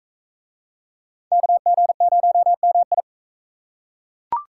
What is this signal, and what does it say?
Morse code, heard as a keyed tone, about 37 wpm, 700 Hz. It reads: KG0MI